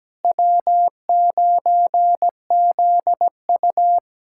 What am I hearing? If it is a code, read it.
Morse code, heard as a keyed tone, about 17 wpm, 700 Hz. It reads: W9ZU
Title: USA Callsign DE Callsign - 4 Characters - 17wpm